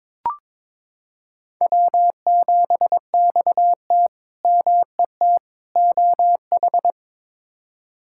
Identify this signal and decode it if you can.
Morse code, heard as a keyed tone, about 22 wpm, 700 Hz. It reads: W7XT MET OH
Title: K1USN SST QSO Practice - 22wpm